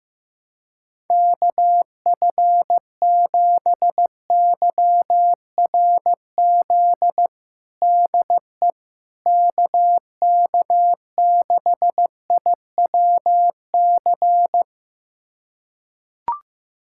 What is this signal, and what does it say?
Morse code, heard as a keyed tone, about 15 wpm, 700 Hz. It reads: KF7YRZ DE KK6IWC
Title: USA Callsign DE Callsign - 6 Characters - 15wpm